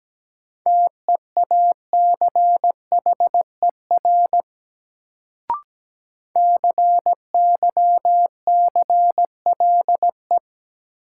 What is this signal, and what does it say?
Morse code, heard as a keyed tone, about 17 wpm, 700 Hz. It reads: TEACHER CYCLE
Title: Sentences from Top 900 Words - New Word Drill - 17wpm